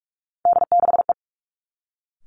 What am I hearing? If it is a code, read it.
Morse code, heard as a keyed tone, about 46 wpm, 700 Hz. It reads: D6E
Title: Three Challenging Letter-Number Combinations 45wpm